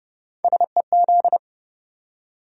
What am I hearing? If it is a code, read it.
Morse code, heard as a keyed tone, about 30 wpm, 700 Hz. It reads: SEZ